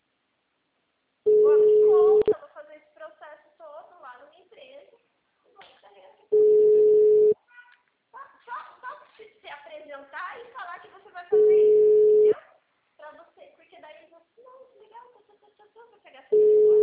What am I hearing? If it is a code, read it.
Ringback tone (Europe, 425 Hz), with speech.